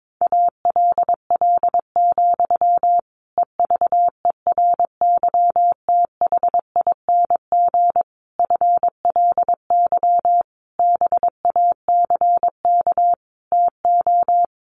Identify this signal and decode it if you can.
Morse code, heard as a keyed tone, about 22 wpm, 700 Hz. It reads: ALL, EVERYTHING FLY BACK TO
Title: A Sound Of Thunder 22wpm